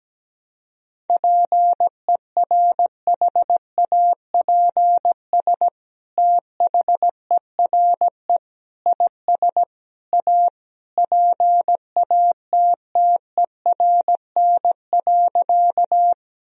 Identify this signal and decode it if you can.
Morse code, heard as a keyed tone, about 17 wpm, 700 Hz. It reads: PERHAPS THERE IS A PATTERN.
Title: Sentences from Top 1000 Words - Review Part 2 - 17wpm